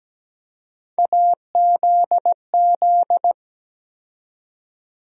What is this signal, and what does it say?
Morse code, heard as a keyed tone, about 17 wpm, 700 Hz. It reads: AZZ
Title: USA Call Sign Postfixes - 3 Letters 17wpm